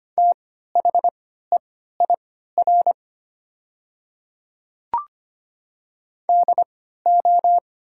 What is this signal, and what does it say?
Morse code, heard as a keyed tone, about 25 wpm, spaced as for 14 wpm, 700 Hz. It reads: THEIR DO
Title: 3x Character Spacing - Top 100 Words - 25wpm